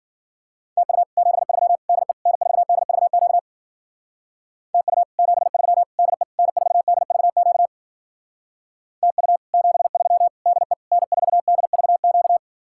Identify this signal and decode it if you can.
Morse code, heard as a keyed tone, about 48 wpm, 700 Hz. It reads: TU 73 DE N4DVQ TU 73 DE N4DVQ TU 73 DE N4DVQ
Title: Speed-Racing Common QSO Phrases - 40wpm